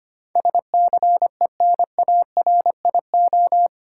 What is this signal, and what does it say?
Morse code, heard as a keyed tone, about 25 wpm, 700 Hz. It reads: SCENARIO
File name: 8 Letter Words - First 3 Letters Given - 25wpm